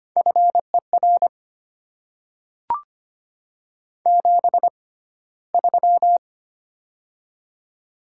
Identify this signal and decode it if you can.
Morse code, heard as a keyed tone, about 25 wpm, spaced as for 15 wpm, 700 Hz. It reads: FER 73
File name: Top 90 QSO Elements - 6x Character Spacing - 25wpm